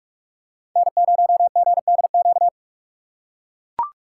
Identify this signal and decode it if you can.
Morse code, heard as a keyed tone, about 44 wpm, 700 Hz. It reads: N0GDQ